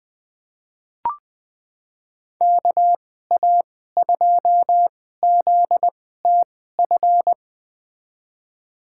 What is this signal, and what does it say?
Morse code, heard as a keyed tone, about 20 wpm, spaced as for 15 wpm, 700 Hz. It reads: KA2ZTF
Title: USA Call Signs - 2x Character Spacing - 20wpm